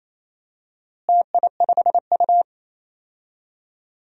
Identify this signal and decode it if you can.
Morse code, heard as a keyed tone, about 28 wpm, 700 Hz. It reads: TI5U
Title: DXCC-100 Call Signs - 4 Characters - Repeated 3x - 28wpm